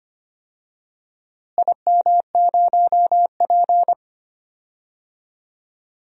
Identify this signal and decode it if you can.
Morse code, heard as a keyed tone, about 25 wpm, 700 Hz. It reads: IM0P
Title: DXCC-100 Call Signs - 4 Characters - Repeated 3x - 25wpm